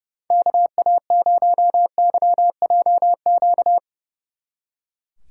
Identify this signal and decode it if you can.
Morse code, heard as a keyed tone, about 39 wpm, 700 Hz. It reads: KA0YJQ